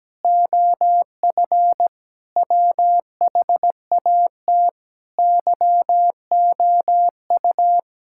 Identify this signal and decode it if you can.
Morse code, heard as a keyed tone, about 17 wpm, 700 Hz. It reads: OF WHAT YOU
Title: Sentences from Top 1600 Words 17wpm